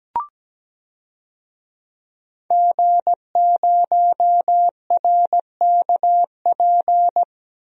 Morse code, heard as a keyed tone, about 17 wpm, 700 Hz. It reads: G0RKP
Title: DXCC-100 Call Signs - 5 Characters - 17wpm